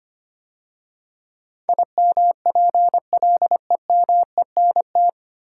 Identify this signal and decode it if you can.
Morse code, heard as a keyed tone, about 25 wpm, 690 Hz. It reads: IMPLEMENT
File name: practice / 25wpm / 9 Letter Words - First 3 Letters Given - 25wpm